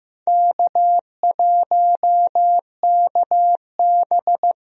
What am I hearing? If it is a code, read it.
Morse code, heard as a keyed tone, about 16 wpm, 700 Hz. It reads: K1KB
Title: USA Call Signs - 4 Characters - 15wpm